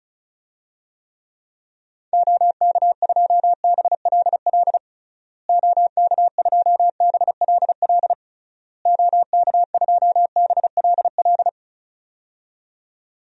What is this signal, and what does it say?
Morse code, heard as a keyed tone, about 36 wpm, 700 Hz. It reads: OK2BLL OK2BLL OK2BLL
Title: DXCC-100 Call Signs - 6 Characters - Repeated 3x - 35wpm